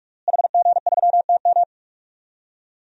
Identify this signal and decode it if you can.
Morse code, heard as a keyed tone, about 46 wpm, 700 Hz. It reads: HG3TM
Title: DXCC-100 Call Signs - 5 Characters - Repeated 2x - 45wpm